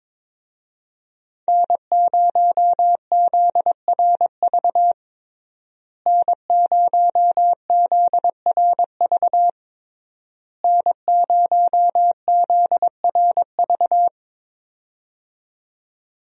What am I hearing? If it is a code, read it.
Morse code, heard as a keyed tone, about 22 wpm, 700 Hz. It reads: N0ZRV N0ZRV N0ZRV
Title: USA Call Signs - 5 Characters - Repeated 3x - 22wpm